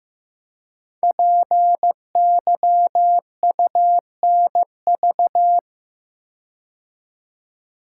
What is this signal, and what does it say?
Morse code, heard as a keyed tone, about 15 wpm, 700 Hz. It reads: PYUNV
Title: Code Groups - 5 Letters - 15wpm